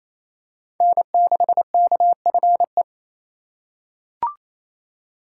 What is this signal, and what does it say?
Morse code, heard as a keyed tone, about 28 wpm, 700 Hz. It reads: N6KFE
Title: USA Call Signs - 5 Characters - 2x Character Spacing - 28wpm